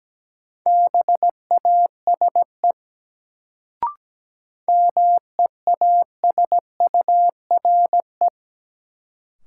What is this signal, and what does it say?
Morse code, heard as a keyed tone, about 17 wpm, 700 Hz. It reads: BASE MEASURE
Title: Sentences from Top 1000 Words - Review Part 1 - New Word Drill - 17wpm